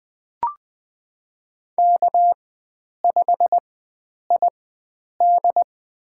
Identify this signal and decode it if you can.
Morse code, heard as a keyed tone, about 20 wpm, spaced as for 9 wpm, 700 Hz. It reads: K5ID
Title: USA Call Signs - 4 Characters - 4x Character Spacing - 20wpm